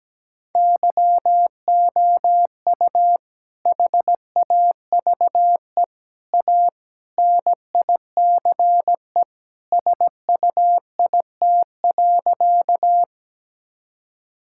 Morse code, heard as a keyed tone, about 17 wpm, 700 Hz. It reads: YOU HAVE A NICE SUIT.
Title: Sentences from Top 1600 Words 17wpm